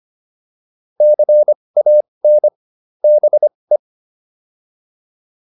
Morse code, heard as a keyed tone, about 25 wpm, spaced as for 20 wpm, 600 Hz. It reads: CAN BE